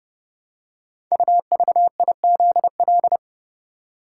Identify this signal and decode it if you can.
Morse code, heard as a keyed tone, about 30 wpm, 700 Hz. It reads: UVIZL
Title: Code Groups - 5 Letters - 30wpm